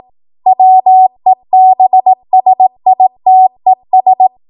Morse code, heard as a keyed tone, about 18 wpm, 750 Hz. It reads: WEBSITES
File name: Morse / 18WPM